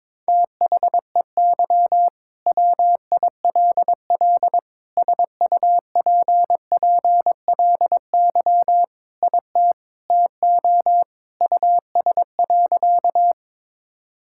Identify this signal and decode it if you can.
Morse code, heard as a keyed tone, about 22 wpm, 700 Hz. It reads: THEY WILL SUPPLY IT TO US.
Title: Sentences from Top 1000 Words - Review Part 5 - 22wpm